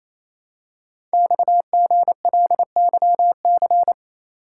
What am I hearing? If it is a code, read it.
Morse code, heard as a keyed tone, about 29 wpm, 700 Hz. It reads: XGLYC
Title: Code Groups - 5 Letters - 28wpm